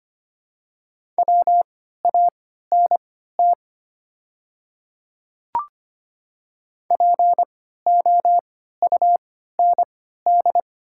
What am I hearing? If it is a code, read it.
Morse code, heard as a keyed tone, about 25 wpm, spaced as for 14 wpm, 690 Hz. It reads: WANT POUND
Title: 3x Character Spacing - Top 500 Words - 25wpm